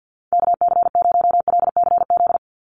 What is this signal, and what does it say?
Morse code, heard as a keyed tone, about 48 wpm, 700 Hz. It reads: KC0LCZ
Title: USA Call Signs 50wpm